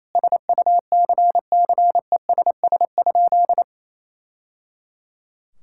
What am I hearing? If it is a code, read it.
Morse code, heard as a keyed tone, about 28 wpm, 700 Hz. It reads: SUCCESS?
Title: Sentences from Top 1000 Words - Review Part 6 - 28wpm